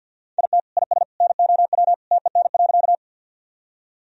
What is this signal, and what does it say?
Morse code, heard as a keyed tone, about 51 wpm, 700 Hz. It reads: IT IS NOW TEN.
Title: Sentences from Top 1000 Words - Review Part 2 - 50wpm